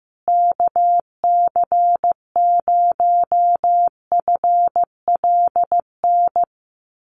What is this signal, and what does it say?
Morse code, heard as a keyed tone, about 15 wpm, 700 Hz. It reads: KC0FLN
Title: USA Call Signs Repeated 3x 15wpm